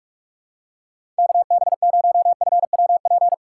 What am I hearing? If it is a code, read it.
Morse code, heard as a keyed tone, about 44 wpm, 700 Hz. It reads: KB0FWP
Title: USA Call Signs - 6 Characters - 6x Character Spacing - 45wpm